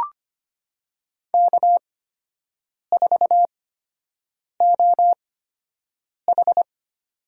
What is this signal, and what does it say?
Morse code, heard as a keyed tone, about 25 wpm, spaced as for 7 wpm, 700 Hz. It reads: K4OH
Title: USA Call Signs - 4 Characters - 8x Character Spacing - 25wpm